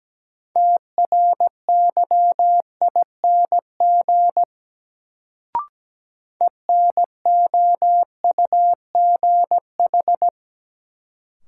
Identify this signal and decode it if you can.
Morse code, heard as a keyed tone, about 17 wpm, 700 Hz. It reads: TRYING ENOUGH
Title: Sets of 2 Words - New Word Drill - 17wpm